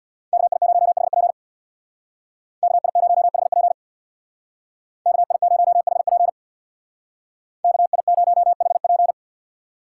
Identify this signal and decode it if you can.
Morse code, heard as a keyed tone, about 58 wpm, 700 Hz. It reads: KI0HP KI0HP KI0HP KI0HP
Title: Speed Racing - USA 2x2 Call Signs 50wpm